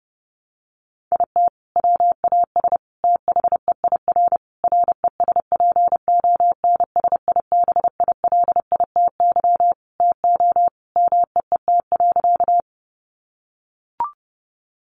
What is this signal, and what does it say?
Morse code, heard as a keyed tone, about 30 wpm, 700 Hz. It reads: IT WAS THEIR RESPONSIBILITY TO MEET.
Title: Sentences from Top 1200 Words 30wpm